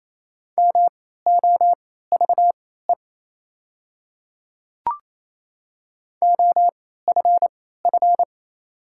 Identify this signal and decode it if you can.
Morse code, heard as a keyed tone, about 28 wpm, spaced as for 16 wpm, 700 Hz. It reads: MOVE OFF
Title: 3x Character Spacing - Top 200 Words - 28wpm